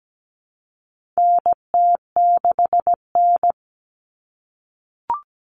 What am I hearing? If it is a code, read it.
Morse code, heard as a keyed tone, about 17 wpm, 700 Hz. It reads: NT6N
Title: All Cut Numbers - 4-Digit 17wpm